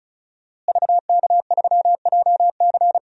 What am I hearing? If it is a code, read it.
Morse code, heard as a keyed tone, about 36 wpm, 700 Hz. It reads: VK3JC
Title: DXCC-100 Call Signs - 5 Characters - Repeated 2x - 35wpm